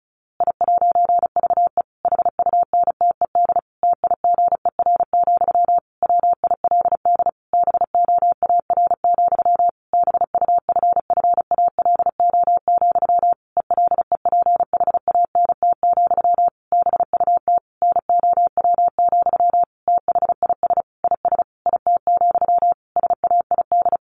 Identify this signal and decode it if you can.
Morse code, heard as a keyed tone, about 34 wpm, 700 Hz. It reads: I'VE HUNTED TIGER, WILD BOAR, BUFFALO, ELEPHANT, BUT NOW, THIS IS IT, SAID
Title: A Sound Of Thunder 35wpm